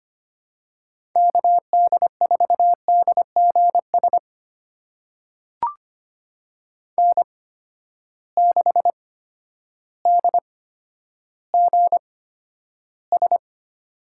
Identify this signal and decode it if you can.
Morse code, heard as a keyed tone, about 25 wpm, spaced as for 11 wpm, 700 Hz. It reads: KD4DGS N6DGS
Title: USA Call Signs - 8x Character Spacing - 25wpm